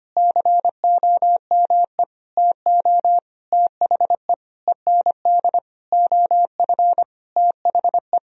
Morse code, heard as a keyed tone, about 25 wpm, 700 Hz. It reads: COME TO THE END OF THE
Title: Sentences from Top 1000 Words - Review Part 2 - 25wpm